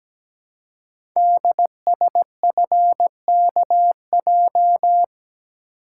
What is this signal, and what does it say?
Morse code, heard as a keyed tone, about 17 wpm, 700 Hz. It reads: DSFKJ
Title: Code Groups - 5 Letters - 17wpm